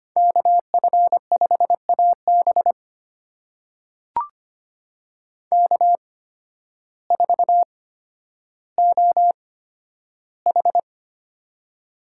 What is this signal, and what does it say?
Morse code, heard as a keyed tone, about 25 wpm, spaced as for 12 wpm, 700 Hz. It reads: KF5AB K4OH
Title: USA Call Signs - 8x Character Spacing - 25wpm